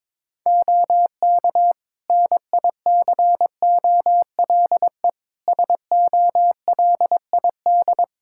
Morse code, heard as a keyed tone, about 22 wpm, 700 Hz. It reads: OK NICOLE SOLID
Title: Practice QSO - Exchange 3 - More Info - 22wpm